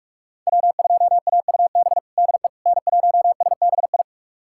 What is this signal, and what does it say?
Morse code, heard as a keyed tone, about 46 wpm, 700 Hz. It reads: W2AUB DE N1SBI
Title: USA Callsign DE Callsign 45wpm